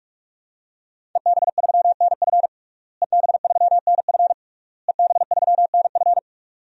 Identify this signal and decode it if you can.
Morse code, heard as a keyed tone, about 46 wpm, 700 Hz. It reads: EB3NF EB3NF EB3NF
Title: DXCC-100 Call Signs - 5 Characters - Repeated 3x - 45wpm